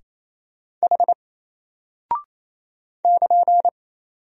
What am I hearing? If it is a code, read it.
Morse code, heard as a keyed tone, about 27 wpm, 700 Hz. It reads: H <KN>
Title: Single Character (Letter:Number:Punctuation:Prosign) Alt 28wpm